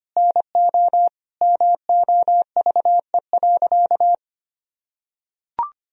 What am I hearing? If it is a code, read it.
Morse code, heard as a keyed tone, about 25 wpm, 700 Hz. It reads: NO MOVE.
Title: Sentences from Top 1000 Words - Review Part 1 - 25wpm